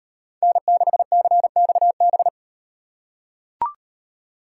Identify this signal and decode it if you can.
Morse code, heard as a keyed tone, about 36 wpm, 700 Hz. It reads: N6CXB